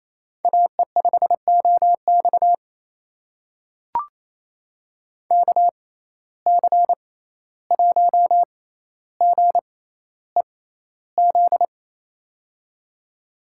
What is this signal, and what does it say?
Morse code, heard as a keyed tone, about 28 wpm, spaced as for 14 wpm, 700 Hz. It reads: AE5OX KC1GEZ
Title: USA Call Signs - 6x Character Spacing - 28wpm